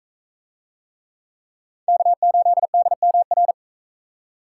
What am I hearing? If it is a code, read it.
Morse code, heard as a keyed tone, about 43 wpm, 700 Hz. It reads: K8DMR